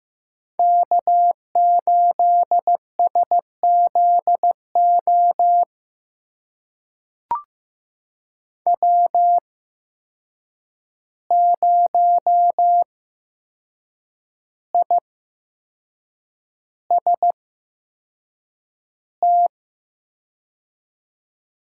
Morse code, heard as a keyed tone, about 15 wpm, spaced as for 6 wpm, 700 Hz. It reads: K8SZO W0IST
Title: USA Call Signs - 5 Characters - 8x Character Spacing - 15wpm